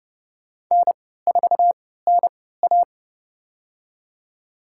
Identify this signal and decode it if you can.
Morse code, heard as a keyed tone, about 30 wpm, spaced as for 17 wpm, 700 Hz. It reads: N4NA
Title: USA Call Signs - 3x Character Spacing - 30wpm